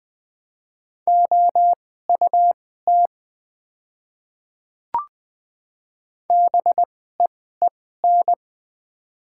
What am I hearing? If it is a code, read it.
Morse code, heard as a keyed tone, about 20 wpm, spaced as for 15 wpm, 700 Hz. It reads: OUT BEEN
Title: 2x Character Spacing - Top 300 Words - 20wpm